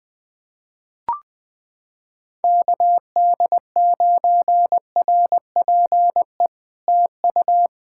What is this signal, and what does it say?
Morse code, heard as a keyed tone, about 20 wpm, 700 Hz. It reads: KD9RPE TU
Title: POTA QSO Practice - 20wpm